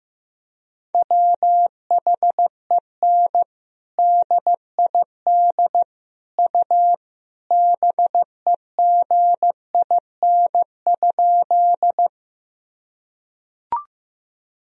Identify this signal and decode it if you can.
Morse code, heard as a keyed tone, about 15 wpm, 700 Hz. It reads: WHEN DID U BEGIN?
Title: Sentences from Top 300 Words and Abbreviations 15wpm